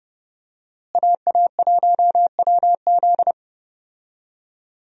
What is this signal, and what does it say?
Morse code, heard as a keyed tone, about 30 wpm, 700 Hz. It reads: AA1WZ
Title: USA Call Signs - 5 Characters - Repeated 2x - 30wpm